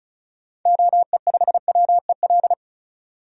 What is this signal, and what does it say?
Morse code, heard as a keyed tone, about 36 wpm, 700 Hz. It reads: OE5WEL
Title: DXCC-100 Call Signs - 6 Characters - 35wpm